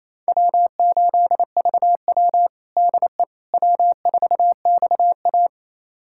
Morse code, heard as a keyed tone, about 28 wpm, 700 Hz. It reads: W8VW DE W4XA
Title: USA Callsign DE Callsign - 4 Characters - 28wpm